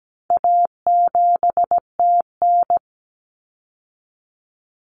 Morse code, heard as a keyed tone, about 17 wpm, 700 Hz. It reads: A7TN
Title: Common Cut Numbers - 4-Digit 17wpm